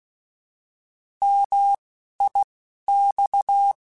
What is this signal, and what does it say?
Morse code, heard as a keyed tone, about 16 wpm, spaced as for 12 wpm, 800 Hz. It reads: MIX